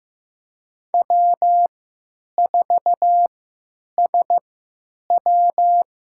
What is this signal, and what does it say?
Morse code, heard as a keyed tone, about 15 wpm, spaced as for 9 wpm, 700 Hz. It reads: W4SW
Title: USA Call Signs - 5 Characters - 3x Character Spacing - 15wpm